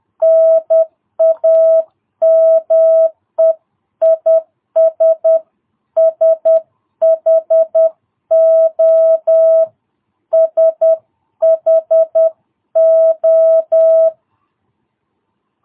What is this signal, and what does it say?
Morse code, heard as a keyed tone, about 10 wpm, 650 Hz. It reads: NAME IS SHO SHO
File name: jh8cbh / 2022cw